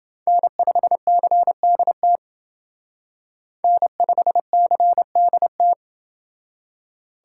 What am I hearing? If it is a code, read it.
Morse code, heard as a keyed tone, about 28 wpm, 700 Hz. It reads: N5CDT N5CDT